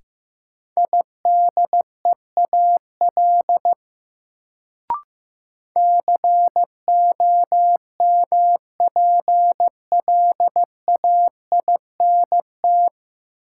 Morse code, heard as a keyed tone, about 15 wpm, 700 Hz. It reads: IDEAL COMPLAINT
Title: Sentences from Top 1400 Words - New Word Drill - 15wpm